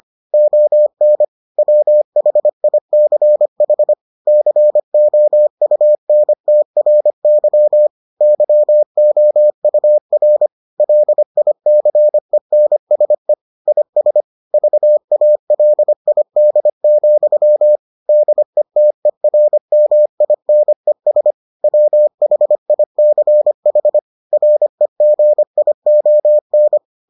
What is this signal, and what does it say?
Morse code, heard as a keyed tone, about 25 wpm, 600 Hz. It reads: ON WHICH COUNTRY YOUR LICENSE IS VALID, DETERMINES WHICH REGION